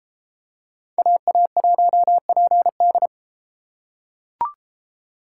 Morse code, heard as a keyed tone, about 33 wpm, 700 Hz. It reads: AA1PD